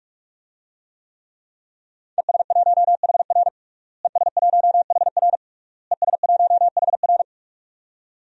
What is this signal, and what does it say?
Morse code, heard as a keyed tone, about 46 wpm, 700 Hz. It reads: ES1HR ES1HR ES1HR
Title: DXCC-100 Call Signs - 5 Characters - Repeated 3x - 45wpm